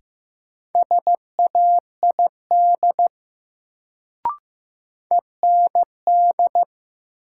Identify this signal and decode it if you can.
Morse code, heard as a keyed tone, about 15 wpm, 700 Hz. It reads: SAID END
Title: Sets of 2 Words - New Word Drill - 15wpm